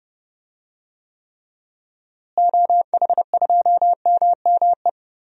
Morse code, heard as a keyed tone, about 30 wpm, 700 Hz. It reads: OH2MME
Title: DXCC-100 Call Signs - 6 Characters - Repeated 3x - 30wpm